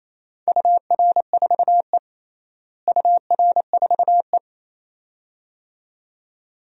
Morse code, heard as a keyed tone, about 28 wpm, 700 Hz. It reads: UR4E UR4E
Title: DXCC-100 Call Signs - 4 Characters - Repeated 2x - 28wpm